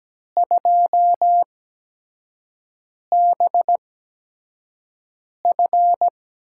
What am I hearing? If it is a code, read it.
Morse code, heard as a keyed tone, about 17 wpm, spaced as for 5 wpm, 700 Hz. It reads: 2BF
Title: USA Call Signs - 4 Characters - 8x Character Spacing - 17wpm